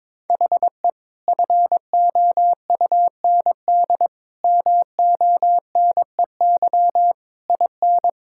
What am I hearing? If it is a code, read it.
Morse code, heard as a keyed tone, about 22 wpm, 700 Hz. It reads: HE FOUND MONEY IN
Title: Sentences from Top 1000 Words - Review Part 6 - 22wpm